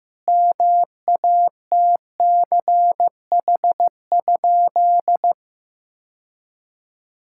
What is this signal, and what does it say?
Morse code, heard as a keyed tone, about 16 wpm, 700 Hz. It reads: MATCH?
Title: Sentences from Top 1600 Words 15wpm